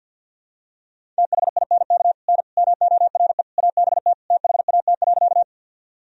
Morse code, heard as a keyed tone, about 49 wpm, 700 Hz. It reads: THINK N MORE ABT THAT.
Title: Sentences from Top 200 Words and Abbreviations 50wpm